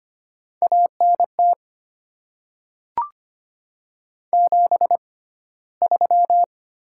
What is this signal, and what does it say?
Morse code, heard as a keyed tone, about 25 wpm, spaced as for 15 wpm, 700 Hz. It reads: ANT 73
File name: Top 20 QSO Elements - 6x Character Spacing - 25wpm